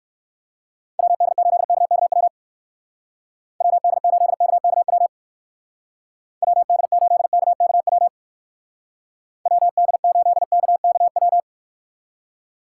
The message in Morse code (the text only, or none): WD8KKW WD8KKW WD8KKW WD8KKW